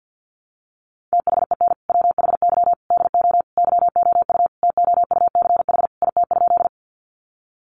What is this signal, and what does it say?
Morse code, heard as a keyed tone, about 50 wpm, 700 Hz. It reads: THEN WHY DO YOU TOUCH IT?
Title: Sentences from Top 1200 Words 50wpm